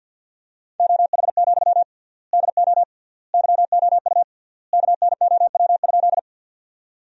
Morse code, heard as a keyed tone, about 50 wpm, 700 Hz. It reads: OH, DO YOU KNOW?
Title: 3x Word-Spacing - Sentences from Top 500 Words - 50wpm